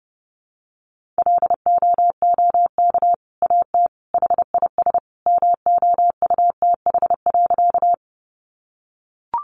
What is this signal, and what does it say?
Morse code, heard as a keyed tone, about 30 wpm, 700 Hz. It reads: LOOK AT HIS MOUTH.
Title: Sentences from Top 1500 Words 30wpm